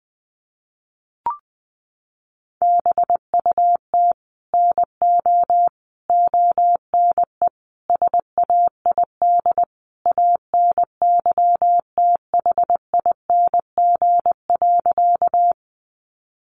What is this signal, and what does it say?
Morse code, heard as a keyed tone, about 20 wpm, 700 Hz. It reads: BUT NO ONE SAID ANYTHING.
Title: Sentences from Top 1000 Words 20wpm